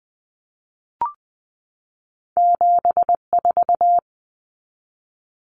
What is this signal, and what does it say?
Morse code, heard as a keyed tone, about 20 wpm, 700 Hz. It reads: 74
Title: Common Cut Numbers - 2-Digit 20wpm